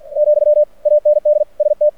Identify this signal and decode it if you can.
Morse code, heard as a keyed tone, about 50 wpm, 600 Hz. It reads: , AND ST
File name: archived / CRSnet